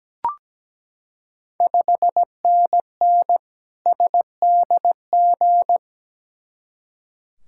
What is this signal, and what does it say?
Morse code, heard as a keyed tone, about 17 wpm, 700 Hz. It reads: 5NN SDG
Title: Ontario QSO Party Exchange 17wpm